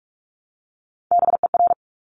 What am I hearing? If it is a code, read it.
Morse code, heard as a keyed tone, about 46 wpm, 700 Hz. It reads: BER
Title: Top Three Letter Combinations in Words 45wpm